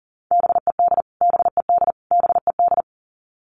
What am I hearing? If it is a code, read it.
Morse code, heard as a keyed tone, about 40 wpm, 700 Hz. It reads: BED BED BED
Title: Top 500 Words Repeated 40wpm